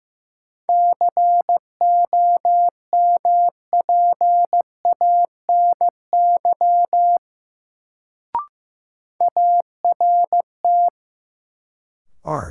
Morse code, heard as a keyed tone, about 15 wpm, 700 Hz. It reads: COMPANY ART
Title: Sentences from Top 600 Words - New Word Drill - 15wpm